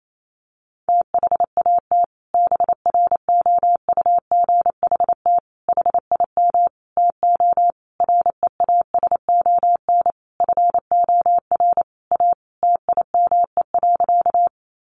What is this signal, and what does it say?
Morse code, heard as a keyed tone, about 28 wpm, 700 Hz. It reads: THAT BROUGHT HIM TO REASON FOR A TIME.
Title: Sentences from Top 700 Words 28wpm